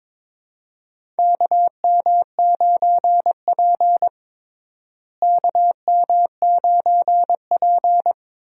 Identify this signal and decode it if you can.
Morse code, heard as a keyed tone, about 21 wpm, 700 Hz. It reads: KM9P KM9P
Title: USA Call Signs - 4 Characters - Repeated 2x - 22wpm